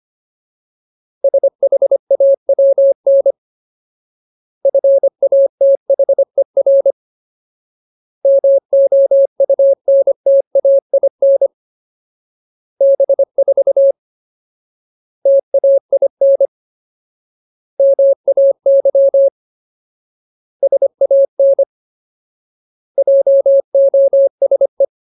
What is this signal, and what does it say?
Morse code, heard as a keyed tone, about 25 wpm, 550 Hz. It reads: SHAWN FATHER MOUNTAIN B4 TAIN MAY SAN JOSE
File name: Ultimate Word List / 25WPM Ultimate